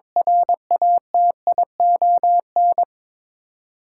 Morse code, heard as a keyed tone, about 21 wpm, 700 Hz. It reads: RATION
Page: morse-code-ninja-practice-22wpm